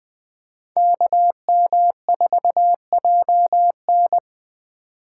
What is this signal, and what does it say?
Morse code, heard as a keyed tone, about 20 wpm, 700 Hz. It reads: KM4JN